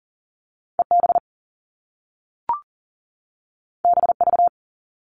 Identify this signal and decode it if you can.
Morse code, heard as a keyed tone, about 42 wpm, 700 Hz. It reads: EB BV
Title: All Cut Numbers - 2-Digit 40wpm